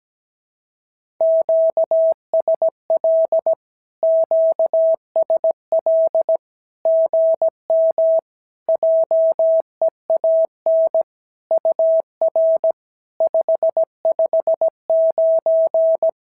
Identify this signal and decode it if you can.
Morse code, heard as a keyed tone, about 17 wpm, 650 Hz. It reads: QSL QSL GM JEAN UR 559
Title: SOTA QSO Practice 17wpm